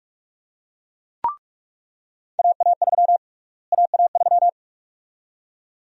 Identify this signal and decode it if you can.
Morse code, heard as a keyed tone, about 46 wpm, 700 Hz. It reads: AA3 AA3
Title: USA Call Sign Prefixes - Repeated 2x - 45wpm